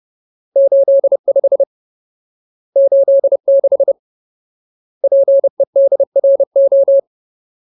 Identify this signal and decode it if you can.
Morse code, heard as a keyed tone, about 30 wpm, 550 Hz. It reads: 85 86 PEDRO
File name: Ultimate Word List / 30WPM Ultimate